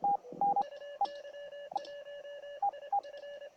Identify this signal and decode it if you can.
Morse code, heard as a keyed tone, about 26 wpm, 1700 Hz, with radio noise. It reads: UY1IF
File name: attachments / EF8RandIR4XQSO